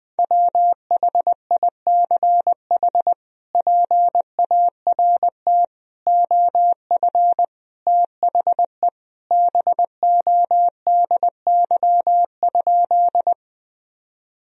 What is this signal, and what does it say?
Morse code, heard as a keyed tone, about 20 wpm, 700 Hz. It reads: WHICH PART OF THE BODY?
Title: Sentences from Top 1000 Words - Review Part 2 - 20wpm